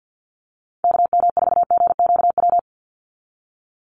Morse code, heard as a keyed tone, about 50 wpm, 700 Hz. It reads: KM4GQW
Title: USA Call Signs 50wpm